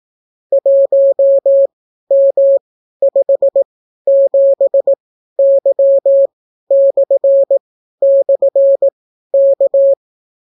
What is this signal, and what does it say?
Morse code, heard as a keyed tone, about 18 wpm, spaced as for 12 wpm, 550 Hz. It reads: 1M57Y//K